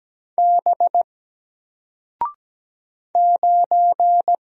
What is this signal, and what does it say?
Morse code, heard as a keyed tone, about 17 wpm, 700 Hz. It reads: B 9
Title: Single Letter-Number Alt 17wpm